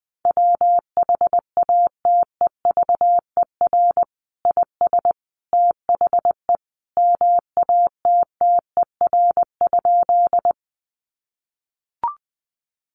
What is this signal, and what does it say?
Morse code, heard as a keyed tone, about 20 wpm, 700 Hz. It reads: WHATEVER IS THE MATTER?
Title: Sentences from Top 1100 Words 20wpm